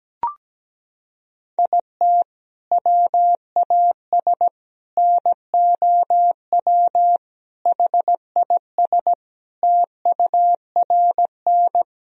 Morse code, heard as a keyed tone, about 17 wpm, 700 Hz. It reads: IT WAS NOW HIS TURN
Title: Sentences from Top 200 Words and Abbreviations 17wpm